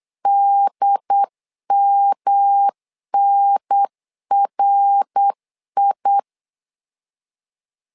Morse code, heard as a keyed tone, about 8 wpm, 800 Hz. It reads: DMNRI
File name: LEZIONI / 3 MWRO / 3 MWR 8wpm 2444